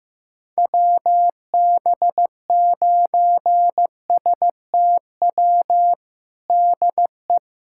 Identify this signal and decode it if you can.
Morse code, heard as a keyed tone, about 15 wpm, 700 Hz. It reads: WB9STW DE
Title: USA Callsign DE Callsign 15wpm